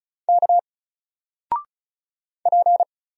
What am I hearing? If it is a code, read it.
Morse code, heard as a keyed tone, about 36 wpm, 700 Hz. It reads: K P
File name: Single-Letter-Number-35wpm